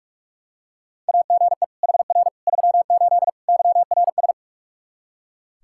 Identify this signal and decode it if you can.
Morse code, heard as a keyed tone, about 42 wpm, 700 Hz. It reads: AGE HR 38 YRS
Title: Practice QSO - Exchange 3 - More Info - 45wpm